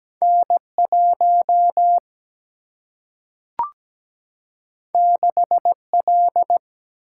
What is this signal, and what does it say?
Morse code, heard as a keyed tone, about 17 wpm, 700 Hz. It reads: N1 6L
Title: Single-Letter Single-Number 17wpm